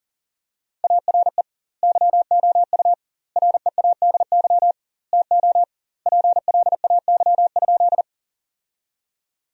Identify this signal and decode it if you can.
Morse code, heard as a keyed tone, about 40 wpm, 700 Hz. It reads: ARE YOU READY TO PLAY?
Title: 2x Word-Spacing - Sentences from Top 400 Words - 40wpm